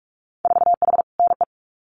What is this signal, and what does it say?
Morse code, heard as a keyed tone, about 46 wpm, 700 Hz. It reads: 4H NE
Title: Winter Field Day Exchanges Encore 45wpm